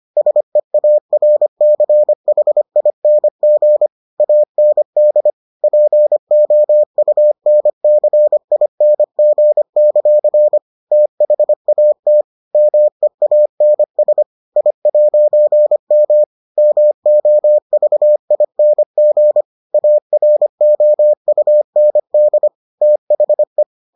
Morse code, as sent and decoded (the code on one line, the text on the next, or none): ... .- .-. -.-. .... .. -. --. / .- -. -.. / .--. --- ..- -. -.-. .. -. --. -.-.-. / - .... .- - / -- . .- -. ... / .. .----. -- / -- --- ...- .. -. --. / .- .-. --- ..- -. -.. / - .... .
SEARCHING AND POUNCING; THAT MEANS I'M MOVING AROUND THE